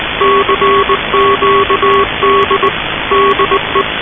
Morse code, heard as a keyed tone, about 19 wpm, 1110 Hz. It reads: CQD DE
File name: luxorion / Radio / titanic-21wpm